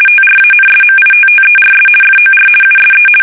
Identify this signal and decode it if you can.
RTTY, about 50 baud. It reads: TRANSMITTERS FOR USE
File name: fsk850-50B / teletype-corp